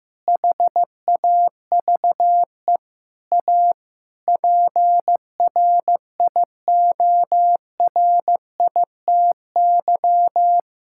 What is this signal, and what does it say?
Morse code, heard as a keyed tone, about 15 wpm, 700 Hz. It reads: HAVE A PRIORITY
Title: Sentences from Top 1600 Words 15wpm